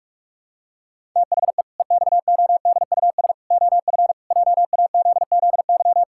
Morse code, heard as a keyed tone, about 46 wpm, 700 Hz. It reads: THE EXODUS OF JAZZY
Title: Pangram Challenge v2 - 45wpm